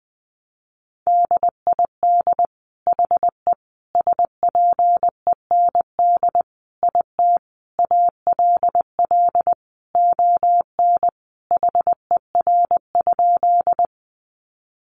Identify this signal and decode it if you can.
Morse code, heard as a keyed tone, about 20 wpm, 700 Hz. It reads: DID HE SPEND IT ALL ON HER?
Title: Sentences from Top 800 Words 20wpm